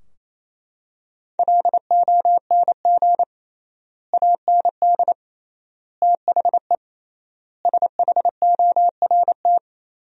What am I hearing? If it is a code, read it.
Morse code, heard as a keyed tone, about 28 wpm, 700 Hz. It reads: LONG AND THE SHORT
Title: 3x Word-Spacing - Sentences from Top 400 Words - 28wpm